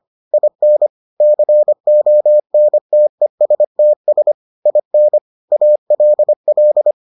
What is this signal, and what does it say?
Morse code, heard as a keyed tone, about 25 wpm, 600 Hz. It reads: IN CONTESTS IN ALL